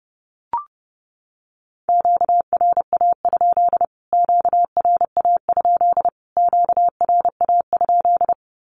Morse code, heard as a keyed tone, about 30 wpm, 700 Hz. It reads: QRA? QRA? QRA?
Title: Q-Codes Repeated 3x 30wpm